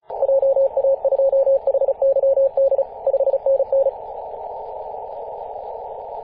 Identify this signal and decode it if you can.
Morse code, heard as a keyed tone, about 34 wpm, 540 Hz. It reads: JA2HYD 5NN
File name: 2016/Apr/09 0838z HV0A 18.071MHz CW